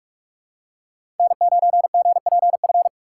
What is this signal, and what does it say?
Morse code, heard as a keyed tone, about 43 wpm, 700 Hz. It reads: N9GPF
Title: USA Call Signs - 5 Characters - 2x Character Spacing - 45wpm